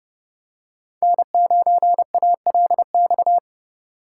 Morse code, heard as a keyed tone, about 30 wpm, 700 Hz. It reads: N9ALX
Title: USA Call Signs - 5 Characters - 4x Character Spacing - 30wpm